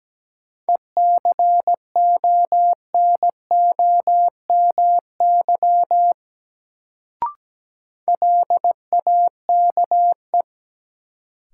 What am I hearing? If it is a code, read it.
Morse code, heard as a keyed tone, about 17 wpm, 700 Hz. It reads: ECONOMY LAKE
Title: Sentences from Top 1000 Words - Review Part 3 - New Word Drill - 17wpm